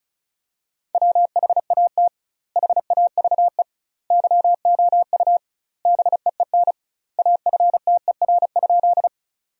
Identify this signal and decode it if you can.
Morse code, heard as a keyed tone, about 36 wpm, 700 Hz. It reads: WHAT HAVE YOU BEEN AFTER?
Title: 2x Word-Spacing - Sentences from Top 200 Words - 35wpm